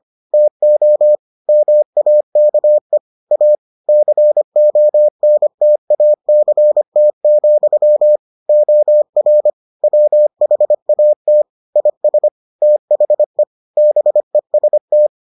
Morse code, heard as a keyed tone, about 25 wpm, 600 Hz. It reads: TO MAKE A CONTACT, OR WHAT IS THE BEST